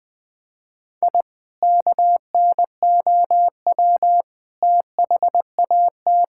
Morse code, heard as a keyed tone, about 20 wpm, 700 Hz. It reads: I KNOW THAT
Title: Sentences from Top 1000 Words - Review Part 6 - 20wpm